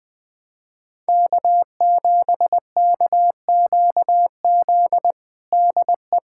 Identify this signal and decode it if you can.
Morse code, heard as a keyed tone, about 21 wpm, 700 Hz. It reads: K7KQZ DE
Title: USA Callsign DE Callsign 20wpm